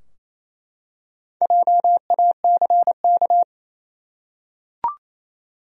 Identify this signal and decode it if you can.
Morse code, heard as a keyed tone, about 27 wpm, 700 Hz. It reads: JACK
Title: USA Common Names - Max 5 Letters - 28wpm